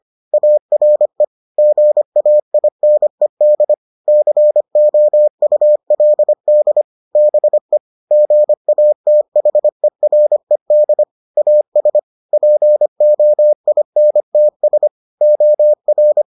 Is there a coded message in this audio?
Yes: Morse code from 0.3 to 16.2 s, heard as a keyed tone, about 25 wpm, 600 Hz. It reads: ARE GAINED COULD BE GATHERED AS POINTS OR